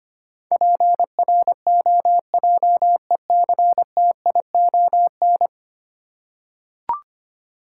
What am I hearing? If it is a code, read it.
Morse code, heard as a keyed tone, about 26 wpm, 700 Hz. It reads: PROJECTION
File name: 10 Letter Words - First 3 Letters Given - 25wpm